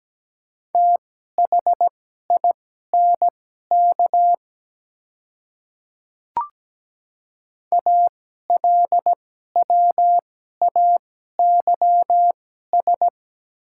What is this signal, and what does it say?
Morse code, heard as a keyed tone, about 17 wpm, spaced as for 12 wpm, 690 Hz. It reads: THINK ALWAYS
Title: 2x Character Spacing - Top 300 Words - 17wpm